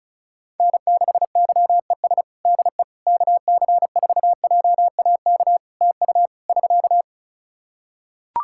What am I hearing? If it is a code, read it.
Morse code, heard as a keyed tone, about 36 wpm, 700 Hz. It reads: N6YES DE KC4JAK TU <SK>
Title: K1USN SST QSO Practice 35wpm